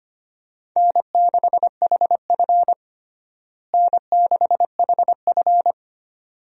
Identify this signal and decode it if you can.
Morse code, heard as a keyed tone, about 25 wpm, 700 Hz. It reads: N6HF N6HF
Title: USA Call Signs - 4 Characters - Repeated 2x - 25wpm